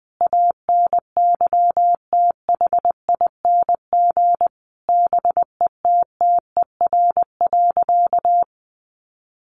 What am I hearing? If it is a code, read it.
Morse code, heard as a keyed tone, about 19 wpm, 700 Hz. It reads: ANYTHING BETTER.
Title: Sentences from Top 1100 Words 20wpm